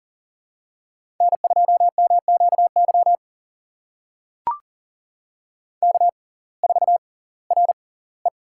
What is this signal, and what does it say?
Morse code, heard as a keyed tone, about 40 wpm, spaced as for 22 wpm, 700 Hz. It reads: N2MQY K4RE
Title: USA Call Signs - 6x Character Spacing - 40wpm